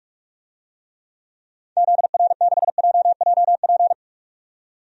Morse code, heard as a keyed tone, about 46 wpm, 700 Hz. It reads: ZR6JJP
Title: DXCC-100 Call Signs - 6 Characters - 45wpm